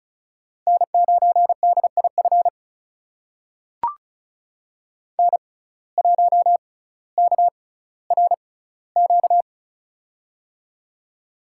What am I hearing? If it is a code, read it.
Morse code, heard as a keyed tone, about 34 wpm, spaced as for 18 wpm, 700 Hz. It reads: N9DIF N1KRQ
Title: USA Call Signs - 5 Characters - 6x Character Spacing - 35wpm